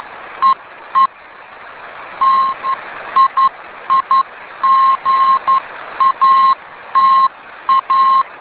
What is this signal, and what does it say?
Morse code, heard as a keyed tone, about 11 wpm, spaced as for 10 wpm, 1070 Hz. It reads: EE TIIGATA